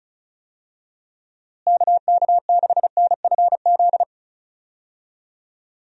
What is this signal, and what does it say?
Morse code, heard as a keyed tone, about 34 wpm, 700 Hz. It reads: KK6NFZ
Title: USA Call Signs - 6 Characters - Repeated 2x - 35wpm